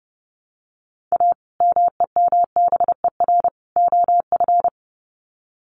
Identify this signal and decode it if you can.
Morse code, heard as a keyed tone, about 30 wpm, 700 Hz. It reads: A MEMBER OF
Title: Sets of 3 Words 30wpm